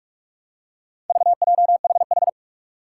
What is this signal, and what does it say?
Morse code, heard as a keyed tone, about 46 wpm, 700 Hz. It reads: VJHH
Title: Four Challenging Letter-Number Combinations 45wpm